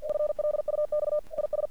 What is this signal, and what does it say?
Morse code, heard as a keyed tone, about 46 wpm, 600 Hz. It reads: FLUX IS
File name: archived / CRSnet